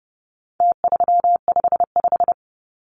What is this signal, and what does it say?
Morse code, heard as a keyed tone, about 30 wpm, 700 Hz. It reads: T355
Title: Numbers - Contest Style - 4-Digit 30wpm